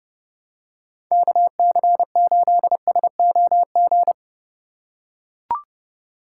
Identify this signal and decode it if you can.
Morse code, heard as a keyed tone, about 29 wpm, 700 Hz. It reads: KC8SOG
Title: USA Call Signs - 2x Character Spacing - 30wpm